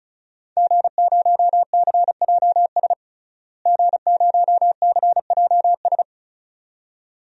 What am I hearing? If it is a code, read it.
Morse code, heard as a keyed tone, about 36 wpm, 700 Hz. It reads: G0CJS G0CJS
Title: DXCC-100 Call Signs - 5 Characters - Repeated 2x - 35wpm